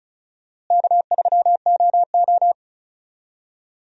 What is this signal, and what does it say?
Morse code, heard as a keyed tone, about 36 wpm, 700 Hz. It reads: K3OO
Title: USA Call Signs - 4 Characters - Repeated 3x - 35wpm